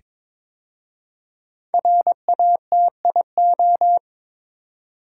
Morse code, heard as a keyed tone, about 22 wpm, 700 Hz. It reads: RATIO
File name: 5 Letter Words - First 3 Letters Given - 22wpm